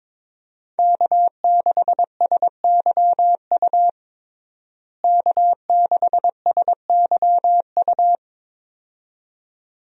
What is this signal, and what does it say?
Morse code, heard as a keyed tone, about 22 wpm, 700 Hz. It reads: K6SYU K6SYU
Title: USA Call Signs - 5 Characters - Repeated 2x - 22wpm